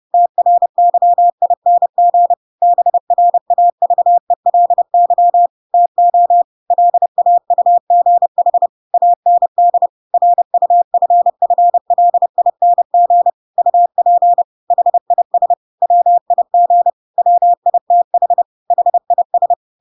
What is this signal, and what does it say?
Morse code, heard as a keyed tone, about 30 wpm, 700 Hz. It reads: TRYING BRAVELY TO LAUGH AND RUFFLING UP HIS WIG WITH HIS
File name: Pinocchio / Pinocchio ch1 30 30 ik1hge